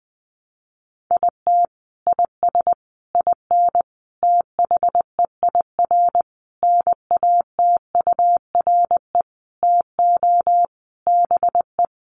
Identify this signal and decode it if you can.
Morse code, heard as a keyed tone, about 20 wpm, 700 Hz. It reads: IT IS IN THEIR NATURE TO BE
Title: Sentences from Top 1300 Words 20wpm